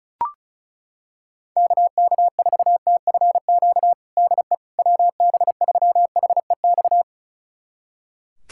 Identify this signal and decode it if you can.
Morse code, heard as a keyed tone, about 34 wpm, 700 Hz. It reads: KK4TFQ DE WB3HEX
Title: USA Callsign DE Callsign - 6 Characters - 35wpm